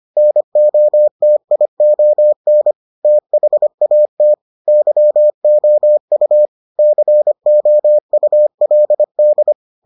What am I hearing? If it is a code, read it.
Morse code, heard as a keyed tone, about 25 wpm, 600 Hz. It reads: NOTION THAT YOU COULD